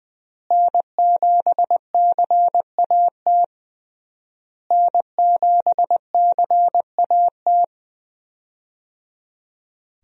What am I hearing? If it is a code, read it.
Morse code, heard as a keyed tone, about 21 wpm, 700 Hz. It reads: N7CAT N7CAT
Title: USA Call Signs - 5 Characters - Repeated 3x - 20wpm